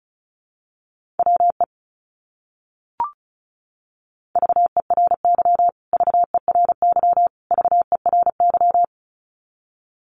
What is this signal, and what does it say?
Morse code, heard as a keyed tone, about 34 wpm, 700 Hz. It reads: WE VERY VERY VERY
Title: Top 200 Words Repeated 35wpm